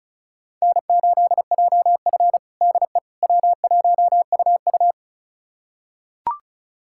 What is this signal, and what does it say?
Morse code, heard as a keyed tone, about 36 wpm, 700 Hz. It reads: N8JF DE W1UU
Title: USA Callsign DE Callsign - 4 Characters - 35wpm